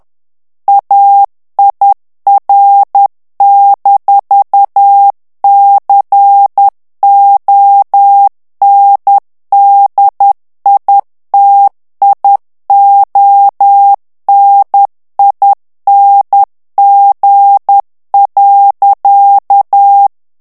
Morse code, heard as a keyed tone, about 10 wpm, 800 Hz. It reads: AIR-CONDITIONING.
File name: morse mp3 / ANGLAIS texte clair 2 vitesse 10 mots minute